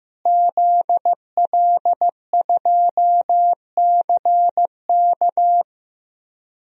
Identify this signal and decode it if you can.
Morse code, heard as a keyed tone, about 15 wpm, 700 Hz. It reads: ZL2CK
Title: DXCC-100 Call Signs - 5 Characters - Repeated 3x - 15wpm